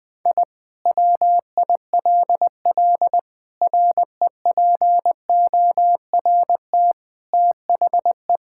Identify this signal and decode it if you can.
Morse code, heard as a keyed tone, about 20 wpm, 700 Hz. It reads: I WILL REPORT THE
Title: Sentences from Top 1000 Words - Review Part 6 - 20wpm